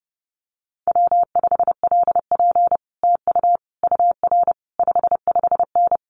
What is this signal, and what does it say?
Morse code, heard as a keyed tone, about 30 wpm, 700 Hz. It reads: W5LP TU UR 55N
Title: Common QSO Phrases Encore 30wpm